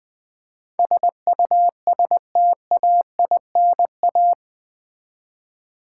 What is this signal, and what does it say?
Morse code, heard as a keyed tone, about 20 wpm, 700 Hz. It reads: SUSTAINA
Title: Build A Word - 4 Letters At A Time - 11-Letter Words - 20wpm